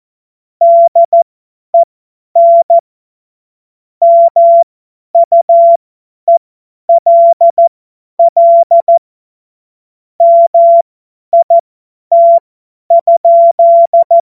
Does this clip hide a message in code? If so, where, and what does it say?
Yes: Morse code from 0.6 to 14.2 s, heard as a keyed tone, about 14 wpm, spaced as for 10 wpm, 680 Hz. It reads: DEN MUELL MIT?